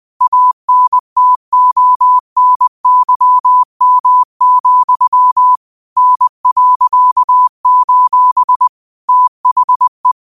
Morse code, heard as a keyed tone, about 20 wpm, 1000 Hz. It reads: ANTONYM, N.: THE